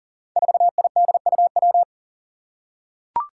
Morse code, heard as a keyed tone, about 39 wpm, 700 Hz. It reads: 4IDUW
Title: Five Challenging Letter-Number Combinations 40wpm